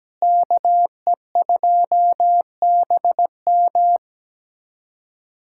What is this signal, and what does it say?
Morse code, heard as a keyed tone, about 17 wpm, 700 Hz. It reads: KE2BM
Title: USA Callsign DE Callsign - 5 Characters - 17wpm